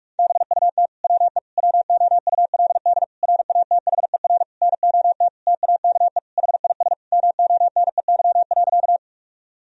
Morse code, heard as a keyed tone, about 46 wpm, 700 Hz. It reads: BUT WE WOULD RATHER NOT TAKE HIS MONEY.
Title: Sentences from Top 1000 Words - Review Part 4 - 45wpm